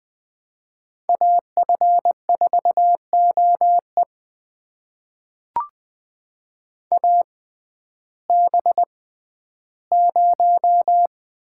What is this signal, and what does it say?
Morse code, heard as a keyed tone, about 21 wpm, spaced as for 12 wpm, 700 Hz. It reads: AF4OE AB0
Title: USA Call Signs - 5 Characters - 6x Character Spacing - 20wpm